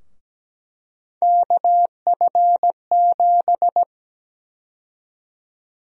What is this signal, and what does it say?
Morse code, heard as a keyed tone, about 17 wpm, 700 Hz. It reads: KF7
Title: USA Call Sign Prefixes - 17wpm